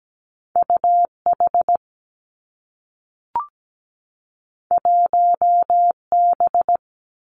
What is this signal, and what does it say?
Morse code, heard as a keyed tone, about 17 wpm, 700 Hz. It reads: U H 1 B
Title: Two Challenging Letter-Number Combinations 17wpm